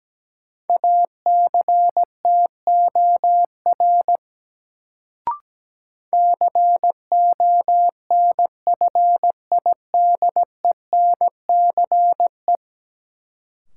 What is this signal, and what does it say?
Morse code, heard as a keyed tone, about 17 wpm, 700 Hz. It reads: ACTOR CONFIDENCE
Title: Sentences from Top 1500 Words - New Word Drill Alt - 17wpm